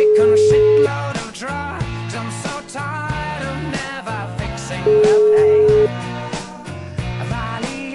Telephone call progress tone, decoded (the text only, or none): ringback tone